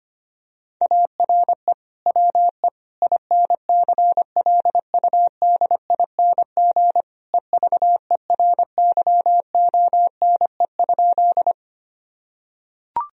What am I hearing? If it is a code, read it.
Morse code, heard as a keyed tone, about 25 wpm, 700 Hz. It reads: ARE WE INCLUDING EVERYONE?
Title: Sentences from Top 1000 Words - Review Part 4 - 25wpm